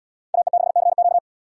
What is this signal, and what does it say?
Morse code, heard as a keyed tone, about 69 wpm, 700 Hz. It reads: AE6CQ